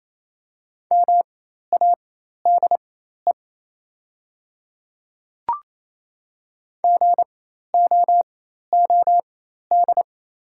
Morse code, heard as a keyed tone, about 27 wpm, spaced as for 13 wpm, 690 Hz. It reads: MADE GOOD
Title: 4x Character Spacing - Top 200 Words - 28wpm